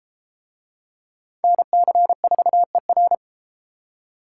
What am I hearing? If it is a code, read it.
Morse code, heard as a keyed tone, about 33 wpm, 700 Hz. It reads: NC4ER